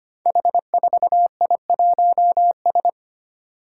Morse code, heard as a keyed tone, about 25 wpm, 700 Hz. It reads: H4I1S
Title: Five Challenging Letter-Number Combinations 25wpm